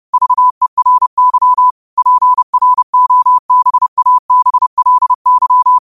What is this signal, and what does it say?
Morse code, heard as a keyed tone, about 30 wpm, 1000 Hz. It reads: VERY PROBABLY
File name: qotdfiles / 30wpm